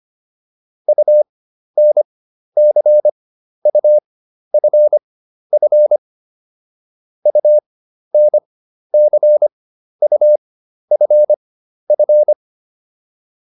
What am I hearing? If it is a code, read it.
Morse code, heard as a keyed tone, about 25 wpm, spaced as for 12 wpm, 600 Hz. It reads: UNCUFF UNCUFF
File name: UN6 12